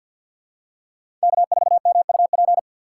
Morse code, heard as a keyed tone, about 50 wpm, 700 Hz. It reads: K4MUP
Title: USA Call Signs - 5 Characters - 4x Character Spacing - 50wpm